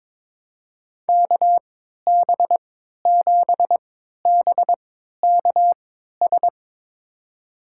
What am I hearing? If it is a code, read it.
Morse code, heard as a keyed tone, about 22 wpm, spaced as for 13 wpm, 700 Hz. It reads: KB7BKS
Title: USA Call Signs - 6 Characters - 3x Character Spacing - 22wpm